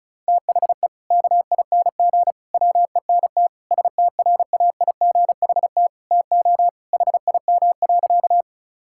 Morse code, heard as a keyed tone, about 36 wpm, 700 Hz. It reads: THE KING WENT STRAIGHT TO HIM.
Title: Sentences from Top 1000 Words - Review Part 6 - 35wpm